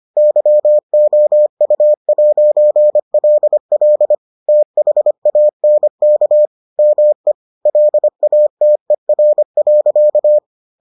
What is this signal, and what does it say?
Morse code, heard as a keyed tone, about 25 wpm, 600 Hz. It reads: YOU'LL THANK ME LATER.